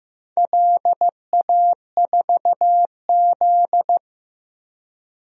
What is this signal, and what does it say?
Morse code, heard as a keyed tone, about 15 wpm, 700 Hz. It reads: LA4Z
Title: DXCC-100 Call Signs - 4 Characters - Repeated 2x - 15wpm